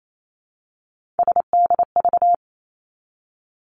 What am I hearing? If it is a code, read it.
Morse code, heard as a keyed tone, about 28 wpm, 700 Hz. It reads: SDV
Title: Three Challenging Letter-Number Combinations 28wpm